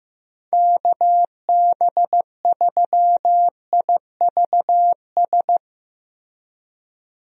Morse code, heard as a keyed tone, about 15 wpm, 700 Hz. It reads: KB3IVS
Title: USA Call Signs - 6 Characters - Repeated 2x - 15wpm